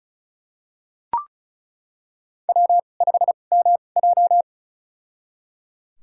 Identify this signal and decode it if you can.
Morse code, heard as a keyed tone, about 36 wpm, spaced as for 25 wpm, 700 Hz. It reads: W5MJ
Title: USA Call Signs - 4 Characters - 2x Character Spacing - 35wpm